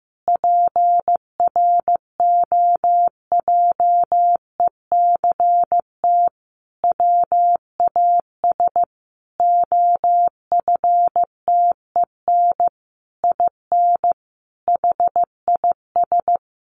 Morse code, heard as a keyed tone, about 15 wpm, 700 Hz. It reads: PROJECT WAS OFTEN IN HIS
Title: Sentences from Top 1400 Words 15wpm